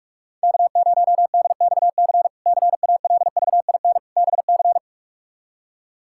Morse code, heard as a keyed tone, about 46 wpm, 700 Hz. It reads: K0DXC CALVIN BC
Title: K1USN SST QSO Practice - 45wpm